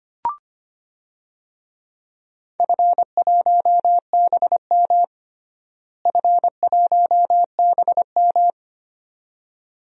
Morse code, heard as a keyed tone, about 25 wpm, 700 Hz. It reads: F1BM F1BM
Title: DXCC-100 Call Signs - 4 Characters - Repeated 2x - 25wpm